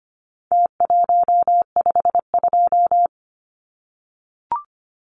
Morse code, heard as a keyed tone, about 25 wpm, 700 Hz. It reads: T152
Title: Numbers - Contest Style - 4-Digit 25wpm